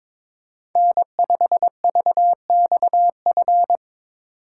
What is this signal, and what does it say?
Morse code, heard as a keyed tone, about 26 wpm, 700 Hz. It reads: N5VXF
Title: Speed Racing - USA Call Signs 22wpm